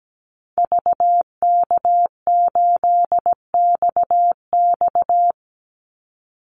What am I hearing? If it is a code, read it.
Morse code, heard as a keyed tone, about 17 wpm, 700 Hz. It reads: VK8XX
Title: Call Signs Repeated 2x - 17wpm